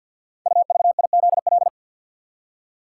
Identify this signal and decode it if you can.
Morse code, heard as a keyed tone, about 47 wpm, 700 Hz. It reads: UVIZL
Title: Code Groups - 5 Letters - 50wpm